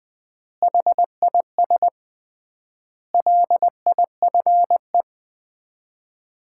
Morse code, heard as a keyed tone, about 20 wpm, 700 Hz. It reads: HIS LIFE
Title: 3x Word-Spacing - Sets of 5 Words - 20wpm